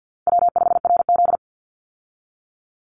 Morse code, heard as a keyed tone, about 50 wpm, 700 Hz. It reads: W5RZ
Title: Easy Call Signs Repeated 2x - 50wpm